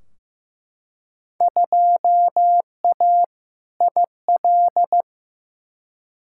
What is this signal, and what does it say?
Morse code, heard as a keyed tone, about 15 wpm, 700 Hz. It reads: 2A IL
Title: Field Day Exchanges Encore - 15wpm